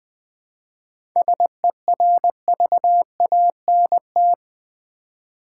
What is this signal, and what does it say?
Morse code, heard as a keyed tone, about 20 wpm, 700 Hz. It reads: SERVANT
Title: 7 Letter Words - First 3 Letters Given - 20wpm